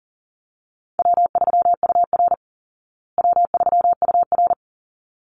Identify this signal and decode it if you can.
Morse code, heard as a keyed tone, about 37 wpm, 700 Hz. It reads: W3UR W3UR
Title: Easy Call Signs Repeated 2x - 40wpm